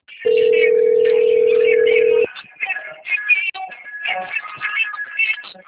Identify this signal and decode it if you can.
Ringback tone (North America), 440 + 480 Hz.
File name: out- - - - - .4961